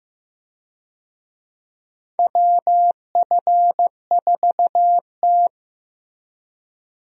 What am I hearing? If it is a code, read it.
Morse code, heard as a keyed tone, about 16 wpm, 700 Hz. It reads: WF4T
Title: USA Call Signs - 4 Characters - Repeated 3x - 15wpm